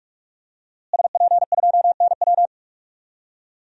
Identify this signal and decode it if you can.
Morse code, heard as a keyed tone, about 46 wpm, 700 Hz. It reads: SP2NW
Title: DXCC-100 Call Signs - 5 Characters - Repeated 2x - 45wpm